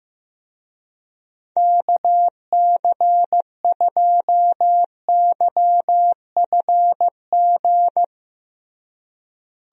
Morse code, heard as a keyed tone, about 15 wpm, 700 Hz. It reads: KC2YFG
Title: USA Call Signs - 6 Characters - Repeated 2x - 15wpm